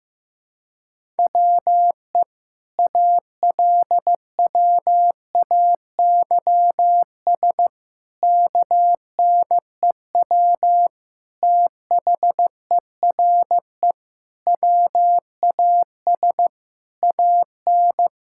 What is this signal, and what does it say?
Morse code, heard as a keyed tone, about 15 wpm, 700 Hz. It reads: WE ALWAYS KNEW THERE WAS AN
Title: Sentences from Top 1000 Words - Review Part 6 - 15wpm